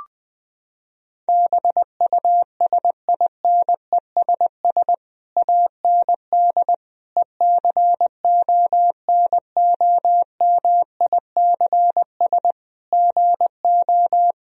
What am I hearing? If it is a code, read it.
Morse code, heard as a keyed tone, about 20 wpm, 700 Hz. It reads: BUSINESS AND ECONOMICS GO
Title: Sentences from Top 1000 Words - Review Part 3 - 20wpm